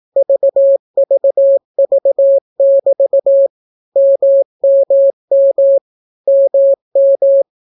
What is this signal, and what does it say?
Morse code, heard as a keyed tone, about 18 wpm, 550 Hz. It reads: VVV<BT> MMM MM